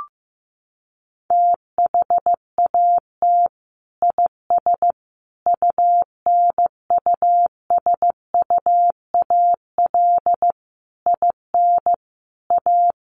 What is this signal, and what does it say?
Morse code, heard as a keyed tone, about 14 wpm, 700 Hz. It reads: THAT IS UNUSUAL IN A
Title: Sentences from Top 1400 Words 15wpm